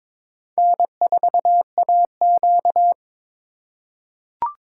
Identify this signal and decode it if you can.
Morse code, heard as a keyed tone, about 21 wpm, 700 Hz. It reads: N4AQ
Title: USA Call Signs - 4 Characters - 4x Character Spacing - 22wpm